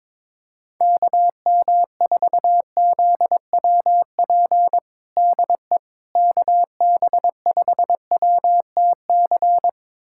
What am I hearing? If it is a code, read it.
Morse code, heard as a keyed tone, about 22 wpm, 700 Hz. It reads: KM4ZWP DE KB5WTC
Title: USA Callsign DE Callsign 22wpm